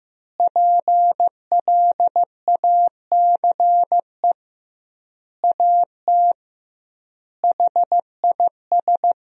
Morse code, heard as a keyed tone, about 15 wpm, 700 Hz. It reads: PLACE AT HIS
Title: 2x Word-Spacing - Sentences from Top 300 Words - 15wpm